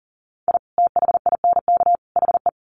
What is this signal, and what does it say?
Morse code, heard as a keyed tone, about 40 wpm, 700 Hz. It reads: I THINK HE
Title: Sets of 3 Words 40wpm